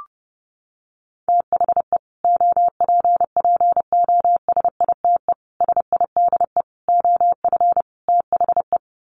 Morse code, heard as a keyed tone, about 30 wpm, 700 Hz. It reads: THE OPPOSITE SIDE OF THE
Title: Sets of 5 Words Encore 30wpm